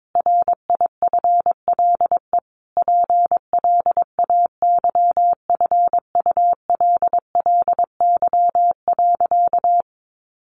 Morse code, heard as a keyed tone, about 22 wpm, 700 Hz. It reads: RIFLE PLAYFULLY.
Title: A Sound Of Thunder 22wpm